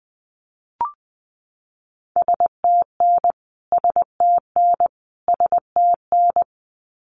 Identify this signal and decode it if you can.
Morse code, heard as a keyed tone, about 20 wpm, 700 Hz. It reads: STN STN STN
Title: Abbreviations Repeated 3x - 20wpm